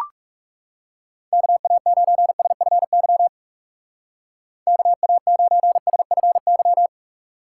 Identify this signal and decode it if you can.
Morse code, heard as a keyed tone, about 36 wpm, 700 Hz. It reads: KA9SFY KA9SFY
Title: Speed Racing - USA 2x3 Call Signs 30wpm